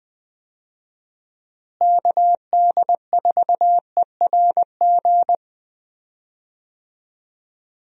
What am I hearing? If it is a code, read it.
Morse code, heard as a keyed tone, about 20 wpm, 700 Hz. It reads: KD4ERG
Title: USA Call Signs - 6 Characters - Repeated 2x - 20wpm